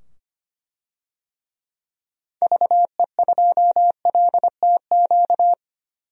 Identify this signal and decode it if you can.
Morse code, heard as a keyed tone, about 25 wpm, 700 Hz. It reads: VE2LTQ
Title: DXCC-100 Call Signs - 6 Characters - 25wpm